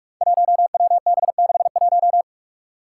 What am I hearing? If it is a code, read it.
Morse code, heard as a keyed tone, about 46 wpm, 700 Hz. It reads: 1WB61
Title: Five Challenging Letter-Number Combinations 45wpm